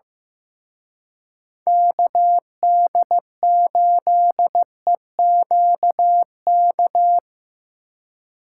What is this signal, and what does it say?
Morse code, heard as a keyed tone, about 17 wpm, 700 Hz. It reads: KD8EQK
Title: Speed Racing - USA 2x3 Call Signs 15wpm